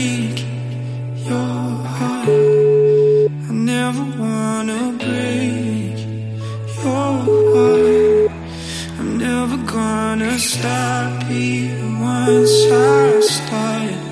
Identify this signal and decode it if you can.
Ringback tone (Europe, 425 Hz), with noise.